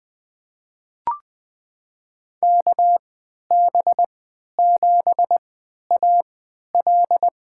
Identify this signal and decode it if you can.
Morse code, heard as a keyed tone, about 20 wpm, spaced as for 11 wpm, 700 Hz. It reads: KB7AL
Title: USA Call Signs - 3x Character Spacing - 20wpm